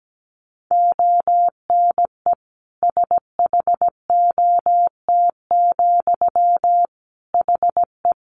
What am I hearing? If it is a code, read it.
Morse code, heard as a keyed tone, about 17 wpm, 700 Hz. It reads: ONE SHOT, HE
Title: Sentences from Top 1100 Words 17wpm